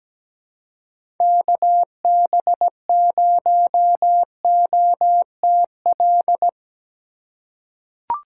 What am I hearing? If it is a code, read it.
Morse code, heard as a keyed tone, about 17 wpm, 700 Hz. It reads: KB0OTL
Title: USA Call Signs - 6 Characters - 2x Character Spacing - 17wpm